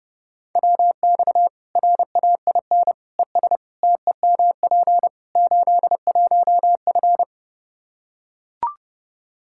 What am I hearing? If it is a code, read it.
Morse code, heard as a keyed tone, about 30 wpm, 700 Hz. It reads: WX RAIN ES TEMP 81F
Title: Practice QSO - Exchange 2 - Info - 30wpm